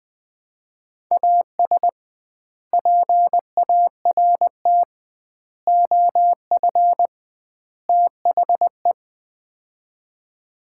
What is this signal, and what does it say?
Morse code, heard as a keyed tone, about 21 wpm, 700 Hz. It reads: AS PART OF THE
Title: 2x Word-Spacing - Sets of 4 Words - 20wpm